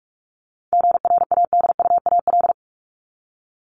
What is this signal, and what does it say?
Morse code, heard as a keyed tone, about 44 wpm, 700 Hz. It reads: GRADUAL
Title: LY Suffix - Top 100 Words with Base Words 45wpm